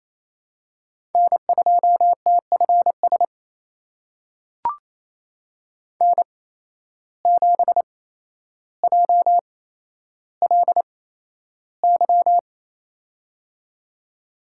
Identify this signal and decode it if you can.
Morse code, heard as a keyed tone, about 27 wpm, spaced as for 12 wpm, 700 Hz. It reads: N2TFS N7JLY
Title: USA Call Signs - 5 Characters - 8x Character Spacing - 28wpm